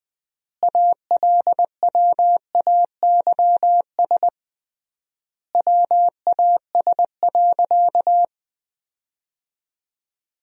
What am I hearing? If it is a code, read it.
Morse code, heard as a keyed tone, about 21 wpm, 700 Hz. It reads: ALWAYS WAS.
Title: 3x Word-Spacing - Sentences from Top 300 Words - 20wpm